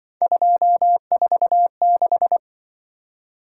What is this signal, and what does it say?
Morse code, heard as a keyed tone, about 23 wpm, 700 Hz. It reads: 246
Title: Speed-Racing - Three Letter-Number Combinations - 20wpm